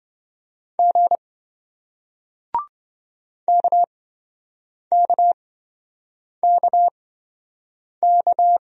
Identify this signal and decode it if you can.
Morse code, heard as a keyed tone, about 28 wpm, spaced as for 7 wpm, 700 Hz. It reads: G KKKK